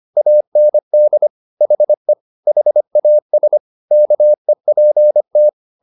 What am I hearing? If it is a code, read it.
Morse code, heard as a keyed tone, about 25 wpm, 600 Hz. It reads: AND HE HAS KEPT